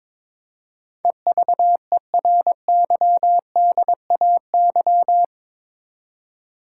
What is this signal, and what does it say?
Morse code, heard as a keyed tone, about 22 wpm, 700 Hz. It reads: EVERYDAY
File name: practice / 22wpm / Eight Letter Words - 22wpm